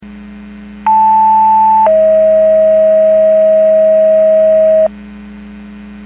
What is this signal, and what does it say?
Two-tone page, heard as a one-second tone, then a three-second tone, about 880 then 640 Hz. It is heard from 0.9 to 4.9 s.
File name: Portage / 06122 02-35 Portage 154.385 Paging